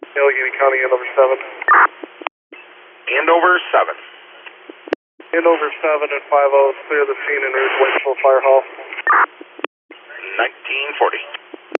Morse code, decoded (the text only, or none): none